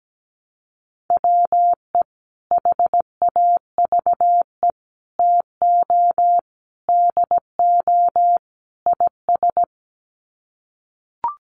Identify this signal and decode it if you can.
Morse code, heard as a keyed tone, about 17 wpm, 700 Hz. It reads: WE HAVE TO DO IS
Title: Sets of 5 Words Encore 17wpm